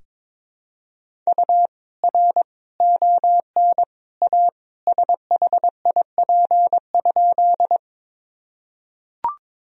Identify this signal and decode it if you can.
Morse code, heard as a keyed tone, about 22 wpm, 700 Hz. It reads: U R ON A SHIP?
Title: Sentences from Top 400 Words and Abbreviations 22wpm